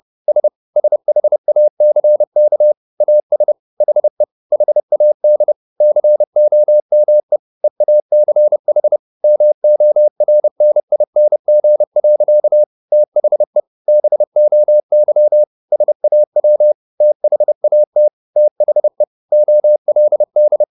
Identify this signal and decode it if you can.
Morse code, heard as a keyed tone, about 30 wpm, 600 Hz. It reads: S SHACK AS HE HAD COME EACH MORNING. THE BOY SAW THAT THE OLD